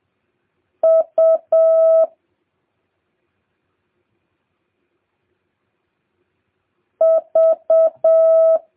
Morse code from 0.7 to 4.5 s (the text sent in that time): U